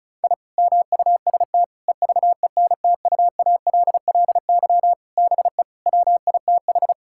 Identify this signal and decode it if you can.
Morse code, heard as a keyed tone, about 36 wpm, 700 Hz. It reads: I MUST EVENTUALLY BE WITH
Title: Sentences from Top 1000 Words - Review Part 6 - 35wpm